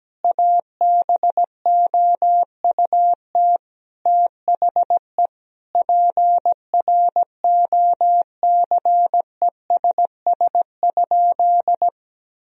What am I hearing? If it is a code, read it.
Morse code, heard as a keyed tone, about 17 wpm, 700 Hz. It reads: ABOUT THE PROCESS?
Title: Sentences from Top 1000 Words - Review Part 3 - 17wpm